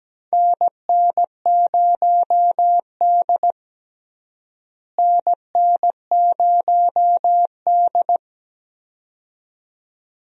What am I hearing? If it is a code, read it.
Morse code, heard as a keyed tone, about 17 wpm, 700 Hz. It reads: NN0D NN0D
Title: USA Call Signs - 4 Characters - Repeated 2x - 17wpm